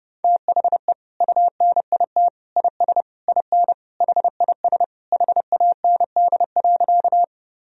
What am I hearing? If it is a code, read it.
Morse code, heard as a keyed tone, about 32 wpm, 700 Hz. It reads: THE UNIT IS IN HIS HAND.
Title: Sentences from Top 1000 Words - Review Part 2 - 30wpm